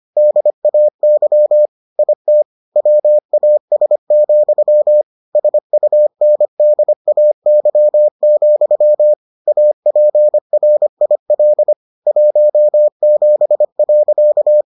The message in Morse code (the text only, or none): DAY IT WAS, SUNDAY, APRIL 17.